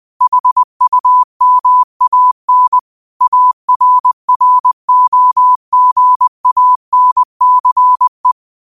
Morse code, heard as a keyed tone, about 20 wpm, 1000 Hz. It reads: HUMAN ARROGANCE